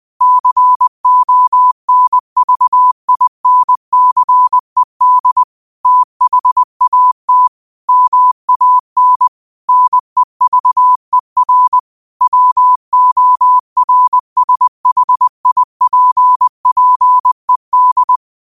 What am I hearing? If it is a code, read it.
Morse code, heard as a keyed tone, about 20 wpm, 1000 Hz. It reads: CONVINCED THAT MAN NEVER WORSHIPPED
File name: qotdfiles / 20wpm